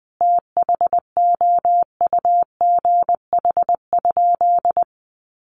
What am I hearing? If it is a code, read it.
Morse code, heard as a keyed tone, about 20 wpm, 700 Hz. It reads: THOUGH?
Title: Sentences from Top 400 Words 20wpm